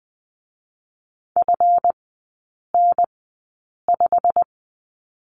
Morse code, heard as a keyed tone, about 20 wpm, spaced as for 8 wpm, 700 Hz. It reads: FN5
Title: 198 Lesson 32e - Letter k - Three Characters 20wpm